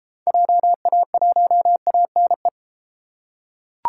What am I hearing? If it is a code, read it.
Morse code, heard as a keyed tone, about 33 wpm, 700 Hz. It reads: JA1ANE